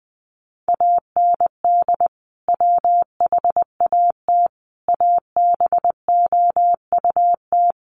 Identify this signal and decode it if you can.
Morse code, heard as a keyed tone, about 20 wpm, 700 Hz. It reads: AND WHAT ABOUT
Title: Sentences from Top 1200 Words 20wpm